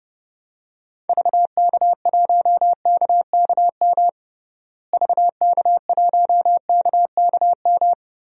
Morse code, heard as a keyed tone, about 30 wpm, 700 Hz. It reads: VK1KKM VK1KKM
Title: DXCC-100 Call Signs - 6 Characters - Repeated 3x - 30wpm